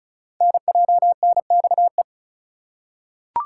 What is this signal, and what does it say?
Morse code, heard as a keyed tone, about 34 wpm, 700 Hz. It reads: NJNXE
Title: Code Groups - 5 Letters - 35wpm